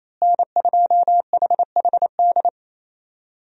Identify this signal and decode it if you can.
Morse code, heard as a keyed tone, about 28 wpm, 700 Hz. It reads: N2HHD
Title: USA Call Signs - 5 Characters - Repeated 2x - 28wpm